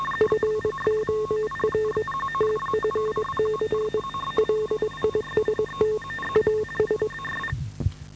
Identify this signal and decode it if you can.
Morse code, heard as a keyed tone, about 22 wpm, 430 Hz, with noise. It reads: FOR TFC LIST AS